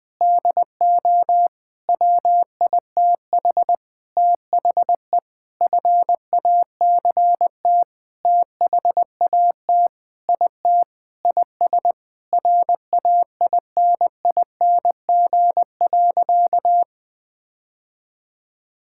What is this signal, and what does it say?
Morse code, heard as a keyed tone, about 20 wpm, 700 Hz. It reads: DO WITH THE FACT THAT IT IS RAINING.
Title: Sentences with 8-Word Phrases - Spoken Beginning - 20wpm